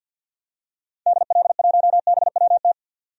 Morse code, heard as a keyed tone, about 52 wpm, 700 Hz. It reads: DL1BWT
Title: DXCC-100 Call Signs - 6 Characters - 50wpm